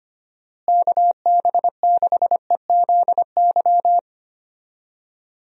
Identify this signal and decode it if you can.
Morse code, heard as a keyed tone, about 25 wpm, 700 Hz. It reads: KB6EZY
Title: USA Call Signs - 6 Characters - Repeated 3x - 25wpm